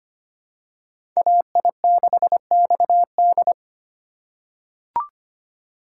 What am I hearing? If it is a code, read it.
Morse code, heard as a keyed tone, about 25 wpm, 700 Hz. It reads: AI6XD
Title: USA Call Signs - 5 Characters - 3x Character Spacing - 25wpm